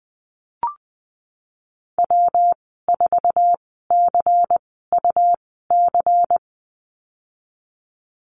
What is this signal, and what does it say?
Morse code, heard as a keyed tone, about 20 wpm, spaced as for 15 wpm, 700 Hz. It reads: W4CUC
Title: 128 Lesson 21g - Number 6 - Call Signs 20wpm